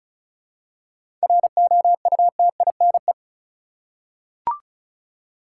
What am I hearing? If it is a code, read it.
Morse code, heard as a keyed tone, about 34 wpm, 700 Hz. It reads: ROUTINE
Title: Seven Letter Words - 35wpm